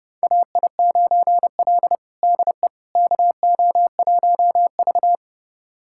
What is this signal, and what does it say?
Morse code, heard as a keyed tone, about 30 wpm, 700 Hz. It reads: AI9L DE KO1V
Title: USA Callsign DE Callsign - 4 Characters - 30wpm